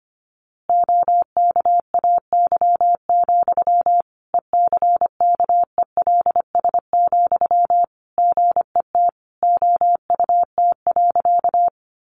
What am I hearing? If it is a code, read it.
Morse code, heard as a keyed tone, about 25 wpm, 700 Hz. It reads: OKAY, ECKELS, GET OUT.
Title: A Sound Of Thunder 25wpm